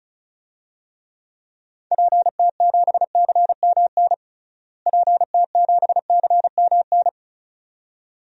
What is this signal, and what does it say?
Morse code, heard as a keyed tone, about 36 wpm, 700 Hz. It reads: PT7CMN PT7CMN
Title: DXCC-100 Call Signs - 6 Characters - Repeated 2x - 35wpm